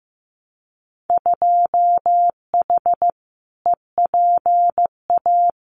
Morse code, heard as a keyed tone, about 15 wpm, 700 Hz. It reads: 2H EPA
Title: Winter Field Day Exchanges Encore 15wpm